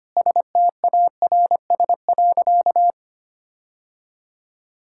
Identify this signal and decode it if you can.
Morse code, heard as a keyed tone, about 25 wpm, 700 Hz. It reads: STARS.
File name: practice / 25wpm / Sentences with 2-Word Phrases - 25wpm